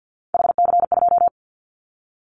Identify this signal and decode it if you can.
Morse code, heard as a keyed tone, about 50 wpm, 690 Hz. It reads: HC2
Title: Callsign Prefixes for Top 100 DXCC - 50wpm